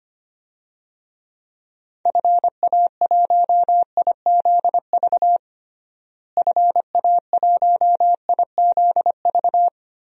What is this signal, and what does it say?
Morse code, heard as a keyed tone, about 25 wpm, 700 Hz. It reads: FA1IZV FA1IZV
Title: DXCC-100 Call Signs - 6 Characters - Repeated 3x - 25wpm